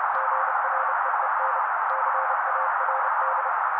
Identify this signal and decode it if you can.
Morse code, heard as a keyed tone, about 30 wpm, 530 Hz. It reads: GREEN CARD
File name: daily / 29wpm noise